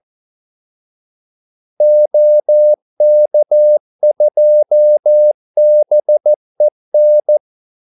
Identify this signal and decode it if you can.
Morse code, heard as a keyed tone, about 14 wpm, 600 Hz. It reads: OK2BEN
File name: Callsigns-14wpm